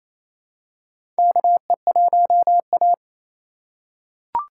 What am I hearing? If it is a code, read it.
Morse code, heard as a keyed tone, about 28 wpm, 700 Hz. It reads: KE1A
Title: USA Call Signs - 4 Characters - 3x Character Spacing - 28wpm